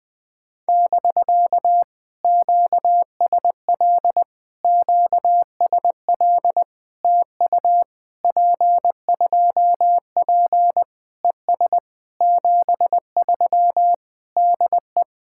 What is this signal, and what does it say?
Morse code, heard as a keyed tone, about 20 wpm, 700 Hz. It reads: <BK> QSL QSL TU P2P ES 73 DE
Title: POTA QSO Practice - 20wpm